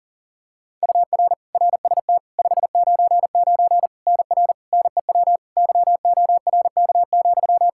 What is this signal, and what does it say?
Morse code, heard as a keyed tone, about 40 wpm, 700 Hz. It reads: UR RST 599 NR NEW YORK,
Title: K1USN SST QSO Practice 40wpm